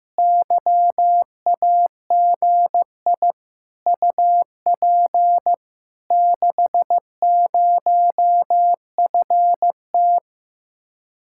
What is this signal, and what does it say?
Morse code, heard as a keyed tone, about 15 wpm, 700 Hz. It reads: YAGI UP 60FT
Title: Speed-Racing Common QSO Phrases Encore - 15wpm